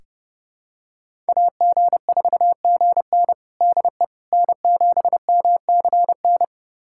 Morse code, heard as a keyed tone, about 30 wpm, 700 Hz. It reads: AG4GN DE N7MCN
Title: USA Callsign DE Callsign - 5 Characters - 30wpm